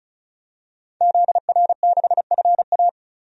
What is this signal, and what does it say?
Morse code, heard as a keyed tone, about 34 wpm, 700 Hz. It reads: ZR6FA
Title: DXCC-100 Call Signs - 5 Characters - 35wpm